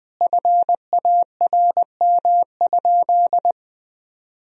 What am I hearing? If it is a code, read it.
Morse code, heard as a keyed tone, about 21 wpm, 700 Hz. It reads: FARM?
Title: Sentences from Top 1000 Words - Review Part 2 - 20wpm